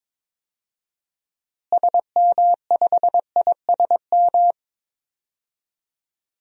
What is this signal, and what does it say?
Morse code, heard as a keyed tone, about 22 wpm, 700 Hz. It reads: SM5ISM
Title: DXCC-100 Call Signs - 6 Characters - Repeated 3x - 22wpm